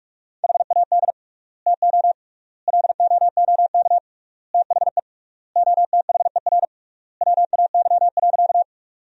Morse code, heard as a keyed tone, about 46 wpm, 700 Hz. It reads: HAD TO LOOK THE OTHER WAY.
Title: 3x Word-Spacing - Sentences from Top 100 Words - 45wpm